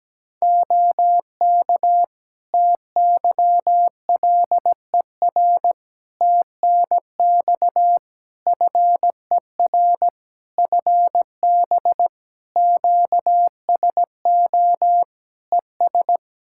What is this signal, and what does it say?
Morse code, heard as a keyed tone, about 17 wpm, 700 Hz. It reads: OK TYLER TNX FER FB QSO ES
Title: Practice QSO - Exchange 4 - Ending - 17wpm